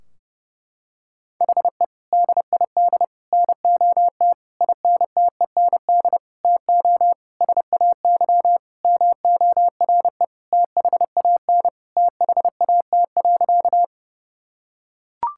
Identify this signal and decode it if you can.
Morse code, heard as a keyed tone, about 30 wpm, 700 Hz. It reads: HE DID NOT INTEND TO SAY MORE THAN THAT.
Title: Sentences from Top 1600 Words 30wpm